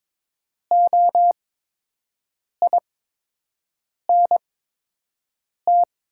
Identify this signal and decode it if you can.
Morse code, heard as a keyed tone, about 22 wpm, spaced as for 6 wpm, 690 Hz. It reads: OINT
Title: 8x Character Spacing - Top 200 Words - 22wpm